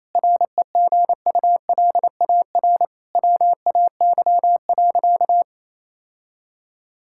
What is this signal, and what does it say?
Morse code, heard as a keyed tone, about 28 wpm, 700 Hz. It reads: REGULAR WAY.
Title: Sentences from Top 1000 Words - Review Part 6 - 28wpm